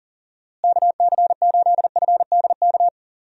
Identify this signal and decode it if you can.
Morse code, heard as a keyed tone, about 40 wpm, 700 Hz. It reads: KC8FDK
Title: USA Call Signs - 6 Characters - Repeated 3x - 40wpm